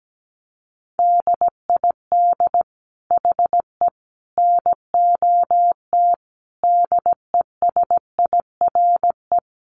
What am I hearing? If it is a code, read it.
Morse code, heard as a keyed tone, about 17 wpm, 700 Hz. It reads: DID HE NOT DESIRE
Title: Sentences from Top 1100 Words 17wpm